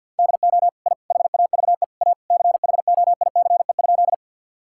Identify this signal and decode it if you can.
Morse code, heard as a keyed tone, about 50 wpm, 700 Hz. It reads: DO I HAVE A CHOICE?
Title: Sentences from Top 1000 Words - Review Part 6 - 50wpm